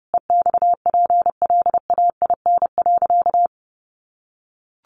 Morse code, heard as a keyed tone, about 30 wpm, 700 Hz. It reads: EXPLAIN.
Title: Sentences from Top 1100 Words 30wpm